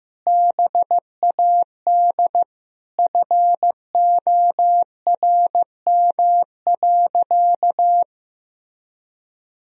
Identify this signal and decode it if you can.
Morse code, heard as a keyed tone, about 15 wpm, 700 Hz. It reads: BAD FORM.
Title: Sentences from Top 1000 Words - Review Part 1 - 15wpm